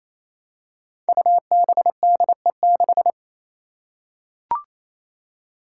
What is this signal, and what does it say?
Morse code, heard as a keyed tone, about 28 wpm, 700 Hz. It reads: UBDE6
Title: Five Challenging Letter-Number Combinations 28wpm